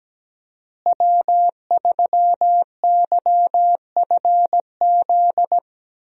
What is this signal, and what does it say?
Morse code, heard as a keyed tone, about 17 wpm, 700 Hz. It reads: W3YFZ
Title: USA Call Signs - 5 Characters - Repeated 3x - 17wpm